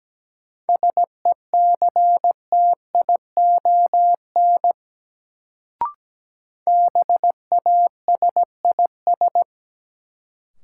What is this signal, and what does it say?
Morse code, heard as a keyed tone, about 17 wpm, 700 Hz. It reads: SECTION BASIS
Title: Sentences from Top 1000 Words - Review Part 5 - New Word Drill - 17wpm